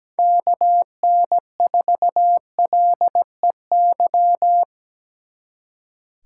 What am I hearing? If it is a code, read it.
Morse code, heard as a keyed tone, about 20 wpm, 700 Hz. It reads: KN4LEY
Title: Speed Racing - USA 2x3 Call Signs 17wpm